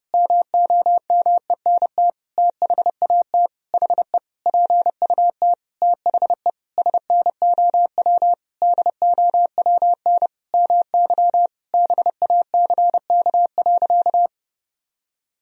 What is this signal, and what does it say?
Morse code, heard as a keyed tone, about 30 wpm, 700 Hz. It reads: MOMENT THAT HE PUT THE SNOW DOWN MY BACK.
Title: Sentences from Top 1000 Words - Review Part 3 - 30wpm